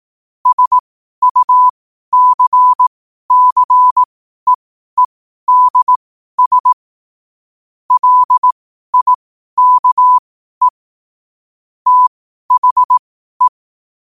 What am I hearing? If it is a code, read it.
Morse code, heard as a keyed tone, about 18 wpm, spaced as for 12 wpm, 1000 Hz. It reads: SUCCEEDS LIKE THE
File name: qotdfiles / 12wpm